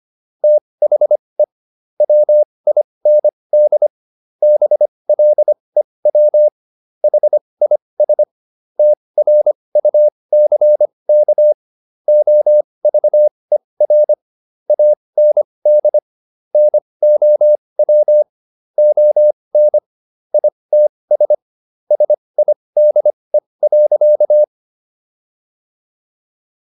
Morse code, heard as a keyed tone, about 25 wpm, spaced as for 20 wpm, 600 Hz. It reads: THE WIND BLEW HIS TRUCK OVER AND NOW ON ITS SIDE.